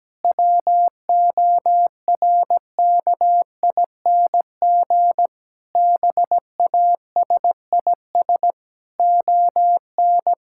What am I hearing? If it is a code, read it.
Morse code, heard as a keyed tone, about 17 wpm, 700 Hz. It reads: WORKING BASIS ON
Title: Sentences from Top 1000 Words - Review Part 5 - 17wpm